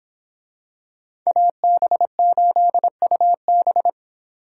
Morse code, heard as a keyed tone, about 24 wpm, 700 Hz. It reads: AB8UB